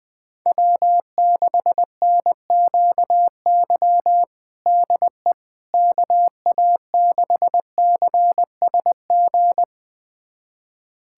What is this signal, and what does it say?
Morse code, heard as a keyed tone, about 20 wpm, 700 Hz. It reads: W6NQY DE KA6CSG
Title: USA Callsign DE Callsign 20wpm